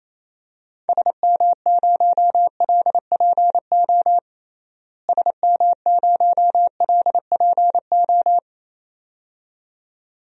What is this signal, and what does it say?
Morse code, heard as a keyed tone, about 28 wpm, 700 Hz. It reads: SM0LPO SM0LPO
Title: DXCC-100 Call Signs - 6 Characters - Repeated 2x - 28wpm